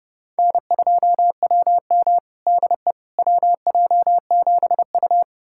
Morse code, heard as a keyed tone, about 30 wpm, 700 Hz. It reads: N2WM DE WJ7U
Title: USA Callsign DE Callsign - 4 Characters - 30wpm